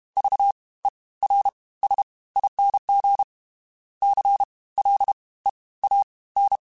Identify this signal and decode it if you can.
Morse code, heard as a keyed tone, about 32 wpm, 800 Hz. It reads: V E R S ING C L E A N
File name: Special INGs / Special INGs-mp3s / Fast ing @30